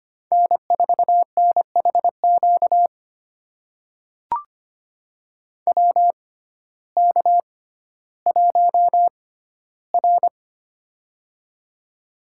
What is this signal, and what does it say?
Morse code, heard as a keyed tone, about 25 wpm, spaced as for 14 wpm, 700 Hz. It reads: N4NHQ WK1R
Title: USA Call Signs - 6x Character Spacing - 25wpm